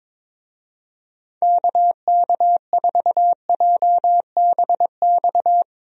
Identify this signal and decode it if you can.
Morse code, heard as a keyed tone, about 22 wpm, 700 Hz. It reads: KK4JBX
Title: USA Call Signs - 6 Characters - Repeated 2x - 22wpm